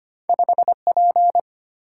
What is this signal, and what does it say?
Morse code, heard as a keyed tone, about 25 wpm, 700 Hz. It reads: 5P
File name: Two Letter-Number Combinations - 25wpm